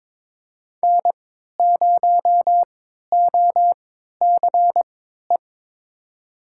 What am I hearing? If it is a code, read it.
Morse code, heard as a keyed tone, about 22 wpm, spaced as for 13 wpm, 700 Hz. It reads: N0OCE
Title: USA Call Signs - 5 Characters - 3x Character Spacing - 22wpm